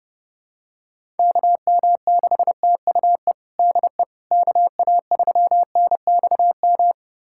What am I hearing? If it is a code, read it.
Morse code, heard as a keyed tone, about 30 wpm, 700 Hz. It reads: KM6TUE DE KA3NXM
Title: USA Callsign DE Callsign 30wpm